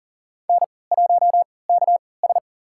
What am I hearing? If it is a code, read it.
Morse code, heard as a keyed tone, about 40 wpm, spaced as for 23 wpm, 700 Hz. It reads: N1KS
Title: USA Call Signs - 4 Characters - 3x Character Spacing - 40wpm